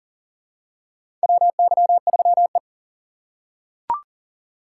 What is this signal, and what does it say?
Morse code, heard as a keyed tone, about 40 wpm, 700 Hz. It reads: WY3E
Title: USA Call Signs - 4 Characters - 4x Character Spacing - 40wpm